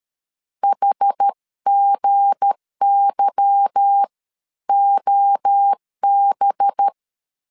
Morse code, heard as a keyed tone, about 13 wpm, spaced as for 11 wpm, 800 Hz. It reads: HGYOB
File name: LEZIONI / 8 YJ45 / 8 Y 12wpm 2228